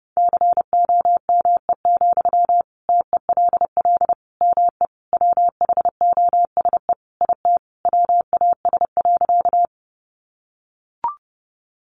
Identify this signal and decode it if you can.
Morse code, heard as a keyed tone, about 30 wpm, 700 Hz. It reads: COME, TELL ME WHOSE IT WAS.
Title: Sentences from Top 1200 Words 30wpm